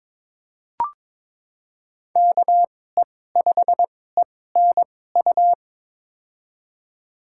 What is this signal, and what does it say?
Morse code, heard as a keyed tone, about 22 wpm, spaced as for 16 wpm, 700 Hz. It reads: KE5ENU
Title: USA Call Signs - 6 Characters - 2x Character Spacing - 22wpm